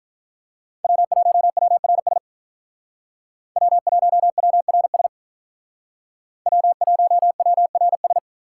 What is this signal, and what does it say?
Morse code, heard as a keyed tone, about 44 wpm, 700 Hz. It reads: W1WRS W1WRS W1WRS
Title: Speed Racing - USA 1x3 Call Signs 35wpm